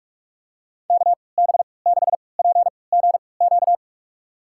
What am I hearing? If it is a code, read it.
Morse code, heard as a keyed tone, about 46 wpm, spaced as for 25 wpm, 700 Hz. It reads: KB6PGQ
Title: USA Call Signs - 3x Character Spacing - 45wpm